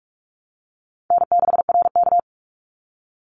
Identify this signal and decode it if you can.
Morse code, heard as a keyed tone, about 46 wpm, 700 Hz. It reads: N6RK
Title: Call Signs - 45wpm